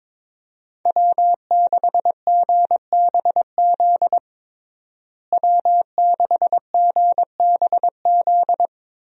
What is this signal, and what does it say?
Morse code, heard as a keyed tone, about 21 wpm, 700 Hz. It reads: W6GBZ W6GBZ
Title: USA Call Signs - 5 Characters - Repeated 2x - 22wpm